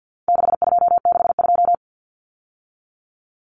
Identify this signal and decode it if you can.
Morse code, heard as a keyed tone, about 50 wpm, 700 Hz. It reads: 6262
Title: Numbers - 4-Digit 50wpm